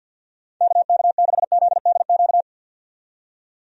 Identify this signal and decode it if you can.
Morse code, heard as a keyed tone, about 48 wpm, 700 Hz. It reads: KK6ZDQ
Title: USA Call Signs - 6 Characters - 50wpm